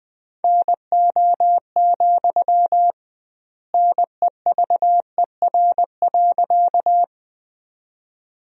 Morse code, heard as a keyed tone, about 20 wpm, 700 Hz. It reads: NO, NEVER.
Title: 2x Word-Spacing - Sentences from Top 300 Words - 20wpm